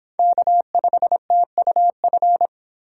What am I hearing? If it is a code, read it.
Morse code, heard as a keyed tone, about 25 wpm, 700 Hz. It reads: K5TUF